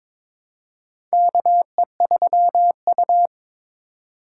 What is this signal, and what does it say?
Morse code, heard as a keyed tone, about 22 wpm, 700 Hz. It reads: KE3U
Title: USA Call Signs - 4 Characters - 3x Character Spacing - 22wpm